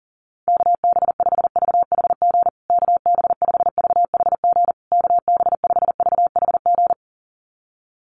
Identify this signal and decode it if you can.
Morse code, heard as a keyed tone, about 37 wpm, 700 Hz. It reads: KB5VHG KB5VHG KB5VHG
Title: USA Call Signs Repeated 3x 40wpm